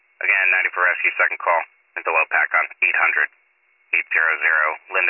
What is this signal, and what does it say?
Emergency Alert System (SAME) tones, heard as harsh buzzing bursts, with speech. It is heard from 0.6 to 4.6 s.